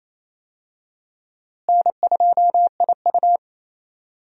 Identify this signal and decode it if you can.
Morse code, heard as a keyed tone, about 28 wpm, 700 Hz. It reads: N2IU
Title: USA Call Signs - 4 Characters - Repeated 2x - 28wpm